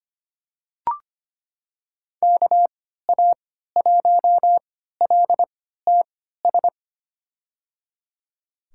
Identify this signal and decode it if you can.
Morse code, heard as a keyed tone, about 25 wpm, spaced as for 14 wpm, 700 Hz. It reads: KA1LTS
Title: USA Call Signs - 6 Characters - 3x Character Spacing - 25wpm